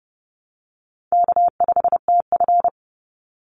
Morse code, heard as a keyed tone, about 30 wpm, 700 Hz. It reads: K5TF
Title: Easy Call Signs Repeated 2x - 30wpm